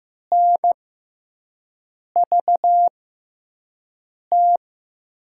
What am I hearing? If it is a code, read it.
Morse code, heard as a keyed tone, about 15 wpm, spaced as for 5 wpm, 700 Hz. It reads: NVT
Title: USA Call Signs - 5 Characters - 6x Character Spacing - 15wpm